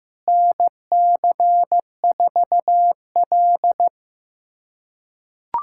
Morse code, heard as a keyed tone, about 15 wpm, 700 Hz. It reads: NC4L
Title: USA Callsign DE Callsign - 4 Characters - 15wpm